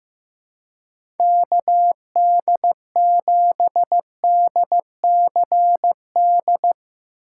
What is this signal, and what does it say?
Morse code, heard as a keyed tone, about 16 wpm, 700 Hz. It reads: KD7DCD
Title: USA Call Signs - 2x Character Spacing - 15wpm